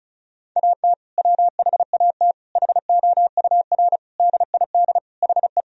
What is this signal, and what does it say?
Morse code, heard as a keyed tone, about 34 wpm, 700 Hz. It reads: AT WHAT HOUR DID HE
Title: Sentences from Top 1000 Words - Review Part 4 - 35wpm